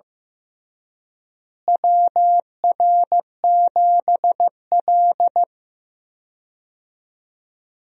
Morse code, heard as a keyed tone, about 15 wpm, 700 Hz. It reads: WR7L
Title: USA Call Signs - 4 Characters - Repeated 3x - 15wpm